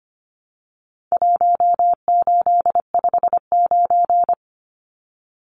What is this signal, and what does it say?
Morse code, heard as a keyed tone, about 25 wpm, 700 Hz. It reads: 1859
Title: Numbers - 4-Digit 25wpm